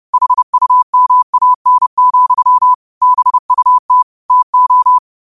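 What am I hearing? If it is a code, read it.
Morse code, heard as a keyed tone, about 30 wpm, 1000 Hz. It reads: HUMAN, BUT TO
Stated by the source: Morse quote of the day at 30 WPM.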